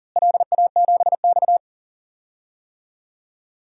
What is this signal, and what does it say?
Morse code, heard as a keyed tone, about 40 wpm, 700 Hz. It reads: LA7X
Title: DXCC-100 Call Signs - 4 Characters - Repeated 3x - 40wpm